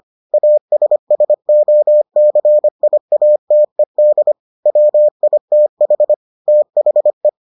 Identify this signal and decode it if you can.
Morse code, heard as a keyed tone, about 25 wpm, 600 Hz. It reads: ASSOCIATED WITH THE